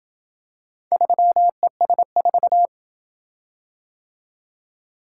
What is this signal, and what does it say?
Morse code, heard as a keyed tone, about 27 wpm, 700 Hz. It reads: 3ES4